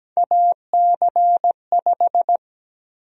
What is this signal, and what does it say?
Morse code, heard as a keyed tone, about 17 wpm, 700 Hz. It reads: AC5
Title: USA Call Sign Prefixes - Repeated 2x - 17wpm